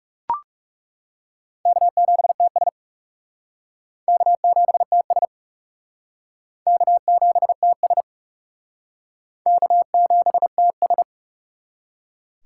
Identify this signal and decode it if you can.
Morse code, heard as a keyed tone, about 38 wpm, 700 Hz. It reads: K7TS K7TS K7TS K7TS
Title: Speed Racing - USA 1x2 Call Signs 30wpm